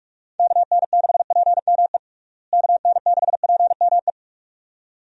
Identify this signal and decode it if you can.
Morse code, heard as a keyed tone, about 41 wpm, 700 Hz. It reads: KN6PME KN6PME
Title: USA Call Signs - 6 Characters - Repeated 2x - 45wpm